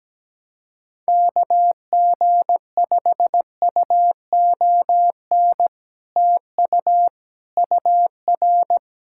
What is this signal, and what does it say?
Morse code, heard as a keyed tone, about 17 wpm, 700 Hz. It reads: KG5UON TU UR
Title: Speed-Racing Common QSO Phrases Encore - 17wpm